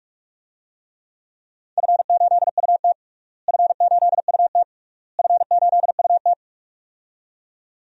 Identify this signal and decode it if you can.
Morse code, heard as a keyed tone, about 46 wpm, 700 Hz. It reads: F8UT F8UT F8UT
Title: DXCC-100 Call Signs - 4 Characters - Repeated 3x - 45wpm